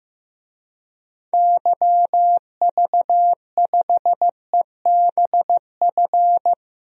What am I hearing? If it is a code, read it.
Morse code, heard as a keyed tone, about 15 wpm, 700 Hz. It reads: YV5EBF
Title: DXCC-100 Call Signs - 6 Characters - Repeated 2x - 15wpm